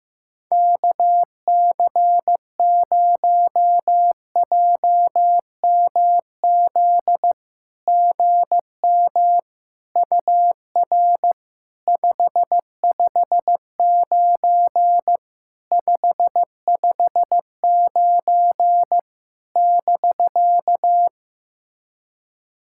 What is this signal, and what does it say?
Morse code, heard as a keyed tone, about 16 wpm, 690 Hz. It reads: KC0JMZ GM UR 559 559 <BK>
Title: SOTA QSO Practice 15wpm